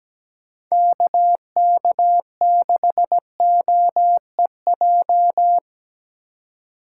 Morse code, heard as a keyed tone, about 17 wpm, 700 Hz. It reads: KK6OEJ
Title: USA Call Signs - 6 Characters - 4x Character Spacing - 17wpm